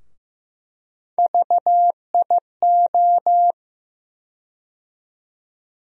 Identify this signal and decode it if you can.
Morse code, heard as a keyed tone, about 15 wpm, 700 Hz. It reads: VIO
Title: USA Call Sign Postfixes - 3 Letters 15wpm